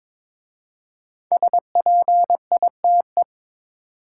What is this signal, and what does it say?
Morse code, heard as a keyed tone, about 22 wpm, 700 Hz. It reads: SPITE
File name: 5 Letter Words - First 3 Letters Given - 22wpm